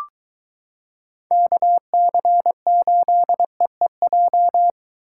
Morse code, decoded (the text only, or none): KC8EEJ